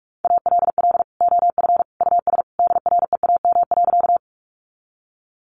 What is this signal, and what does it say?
Morse code, heard as a keyed tone, about 44 wpm, 700 Hz. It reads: ALL OF US DREAM.
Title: Sentences from Top 1500 Words 45wpm